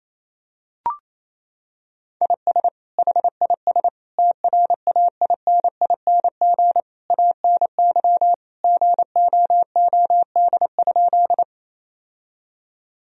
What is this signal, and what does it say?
Morse code, heard as a keyed tone, about 28 wpm, 700 Hz. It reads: IS HIS TRAINING ANY GOOD?
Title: Sentences from Top 1000 Words - Review Part 4 - 28wpm